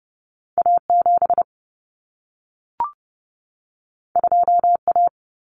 Morse code, heard as a keyed tone, about 30 wpm, 700 Hz. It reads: A7 2A
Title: Common Cut Numbers - 2-Digit 30wpm